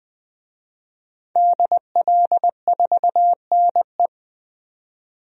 Morse code, heard as a keyed tone, about 20 wpm, 700 Hz. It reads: DL4NE
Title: DXCC-100 Call Signs - 5 Characters - 20wpm